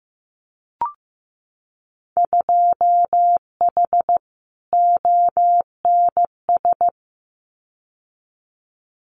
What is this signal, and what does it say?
Morse code, heard as a keyed tone, about 15 wpm, 700 Hz. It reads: 2H ONS
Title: Winter Field Day Exchanges Encore 15wpm